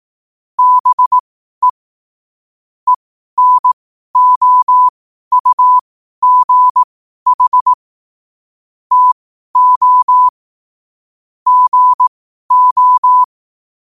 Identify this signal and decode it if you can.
Morse code, heard as a keyed tone, about 18 wpm, spaced as for 12 wpm, 1000 Hz. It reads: BE ENOUGH TO GO
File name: qotdfiles / 12wpm